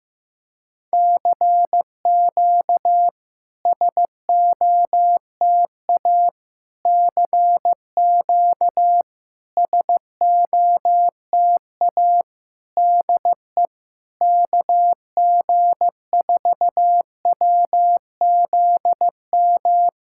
Morse code, heard as a keyed tone, about 15 wpm, 700 Hz. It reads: CQ SOTA CQ SOTA DE KG4WZM
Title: SOTA QSO Practice 15wpm